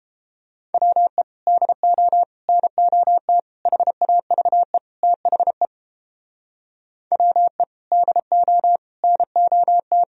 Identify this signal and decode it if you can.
Morse code, heard as a keyed tone, about 31 wpm, 700 Hz. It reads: WE DO NOT HAVE THE WE DO NOT